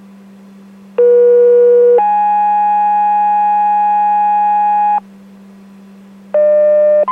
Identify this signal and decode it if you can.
Two-tone page, heard as a one-second tone, then a three-second tone, about 480 then 830 Hz. It is heard from 1.0 to 5.0 s.